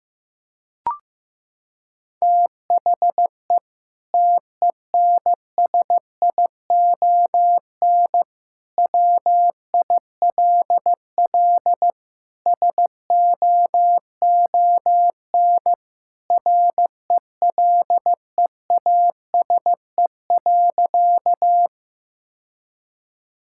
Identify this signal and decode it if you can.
Morse code, heard as a keyed tone, about 15 wpm, 700 Hz. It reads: THE TENSION WILL SOON RELEASE.
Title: Sentences from Top 1600 Words 15wpm